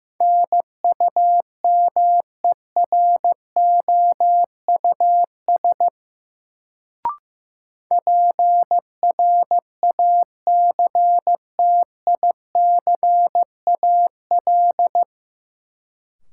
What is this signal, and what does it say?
Morse code, heard as a keyed tone, about 15 wpm, 700 Hz. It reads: NUMEROUS PRACTICAL
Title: Sentences from Top 1400 Words - New Word Drill - 15wpm